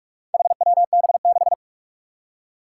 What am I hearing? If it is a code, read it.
Morse code, heard as a keyed tone, about 44 wpm, 700 Hz. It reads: HWB6
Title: Four Challenging Letter-Number Combinations 45wpm